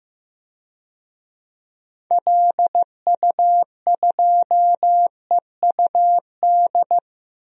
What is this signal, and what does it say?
Morse code, heard as a keyed tone, about 15 wpm, 700 Hz. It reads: LU2EUD
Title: DXCC-100 Call Signs - 6 Characters - 15wpm